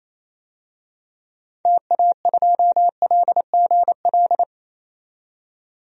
Morse code, heard as a keyed tone, about 28 wpm, 700 Hz. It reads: TA2LGL
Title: DXCC-100 Call Signs - 6 Characters - 28wpm